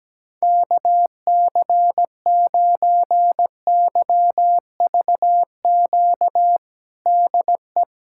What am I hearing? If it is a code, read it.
Morse code, heard as a keyed tone, about 17 wpm, 700 Hz. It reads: KC9YVQ DE
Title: USA Callsign DE Callsign - 6 Characters - 17wpm